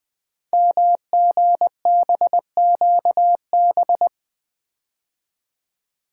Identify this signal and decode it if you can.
Morse code, heard as a keyed tone, about 20 wpm, 700 Hz. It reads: MGBQB
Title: Code Groups - 5 Letters - 20wpm